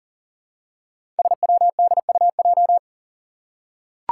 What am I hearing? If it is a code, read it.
Morse code, heard as a keyed tone, about 40 wpm, 700 Hz. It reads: SWDUJ
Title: Five Challenging Letter-Number Combinations 40wpm